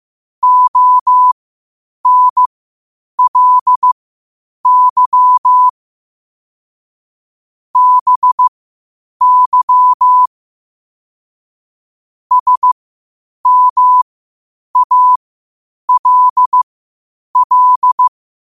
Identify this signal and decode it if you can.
Morse code, heard as a keyed tone, about 15 wpm, spaced as for 8 wpm, 1000 Hz. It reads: ONLY BY SMALL
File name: qotdfiles / 8wpm